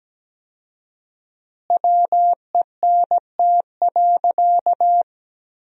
Morse code, heard as a keyed tone, about 17 wpm, 700 Hz. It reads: WENT.
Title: 4x Word-Spacing - Sentences from Top 200 Words - 17wpm